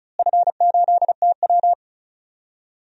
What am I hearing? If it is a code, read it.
Morse code, heard as a keyed tone, about 35 wpm, 700 Hz. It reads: F8TW
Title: DXCC-100 Call Signs - 4 Characters - Repeated 2x - 35wpm